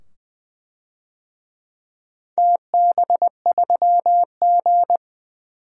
Morse code, heard as a keyed tone, about 20 wpm, 700 Hz. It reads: TB3G
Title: DXCC-100 Call Signs - 4 Characters - 20wpm